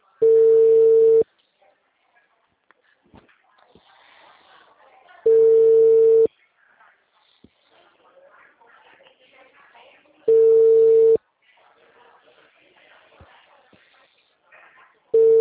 Ringback tone (China, 450 Hz).